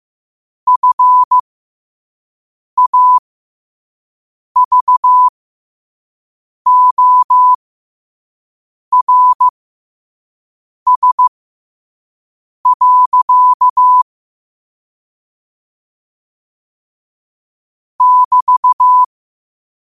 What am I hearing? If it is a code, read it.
Morse code, heard as a keyed tone, about 15 wpm, spaced as for 5 wpm, 1000 Hz. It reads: FAVORS. <BT>
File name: 5wpm